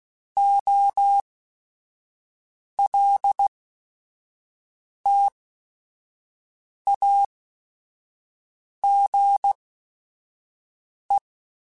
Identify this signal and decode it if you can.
Morse code, heard as a keyed tone, about 16 wpm, spaced as for 5 wpm, 800 Hz. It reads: OLTAGE
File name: Technical Subjects / Technical Subjects-mp3s / XL @5